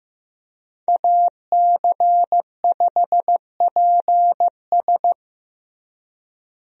Morse code, heard as a keyed tone, about 15 wpm, 700 Hz. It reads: AC5PS
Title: USA Call Signs - 5 Characters - Repeated 3x - 15wpm